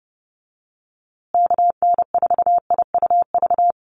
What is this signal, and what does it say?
Morse code, heard as a keyed tone, about 30 wpm, 700 Hz. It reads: KN4IUV
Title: USA Call Signs Repeated 3x 30wpm